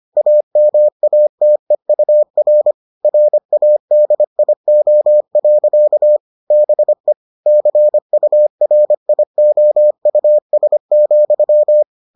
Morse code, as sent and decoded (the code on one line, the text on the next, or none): .- -- .- - . ..- .-. / .-. .- -.. .. --- .-.-.- / -... / -.-. ..- .-. .. --- ..- ... --..--
AMATEUR RADIO. BE CURIOUS,